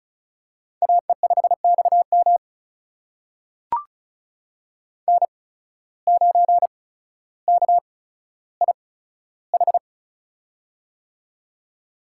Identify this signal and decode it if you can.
Morse code, heard as a keyed tone, about 36 wpm, spaced as for 15 wpm, 700 Hz. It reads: AE5XM N9KIH
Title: USA Call Signs - 5 Characters - 8x Character Spacing - 35wpm